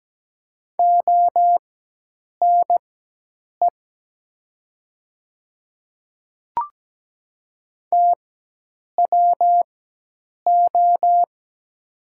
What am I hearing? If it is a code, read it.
Morse code, heard as a keyed tone, about 17 wpm, spaced as for 8 wpm, 700 Hz. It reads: ONE TWO
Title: 4x Character Spacing - Top 100 Words - 17wpm